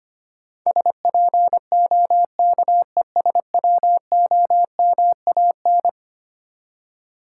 Morse code, heard as a keyed tone, about 25 wpm, 700 Hz. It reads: SPOKESWOMAN
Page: morse-code-ninja-practice-25wpm